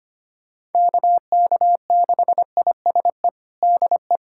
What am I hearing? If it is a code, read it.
Morse code, heard as a keyed tone, about 25 wpm, 700 Hz. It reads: KK6ISE DE
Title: USA Callsign DE Callsign - 6 Characters - 25wpm